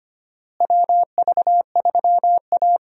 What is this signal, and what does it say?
Morse code, heard as a keyed tone, about 25 wpm, 700 Hz. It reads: WV3A
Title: USA Call Signs - 4 Characters - Repeated 3x - 25wpm